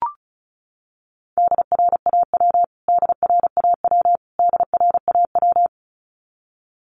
Morse code, heard as a keyed tone, about 34 wpm, 700 Hz. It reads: DRAW DRAW DRAW
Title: Top 500 Words Repeated 35wpm